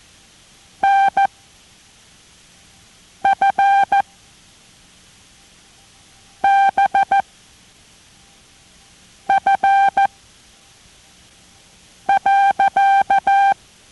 Morse code, heard as a keyed tone, about 14 wpm, 790 Hz. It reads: N F B F .